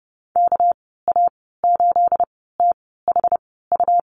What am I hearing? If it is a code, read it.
Morse code, heard as a keyed tone, about 30 wpm, spaced as for 17 wpm, 700 Hz. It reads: KA8THU
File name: Lesson 37g - Letter x - Call Signs 30wpm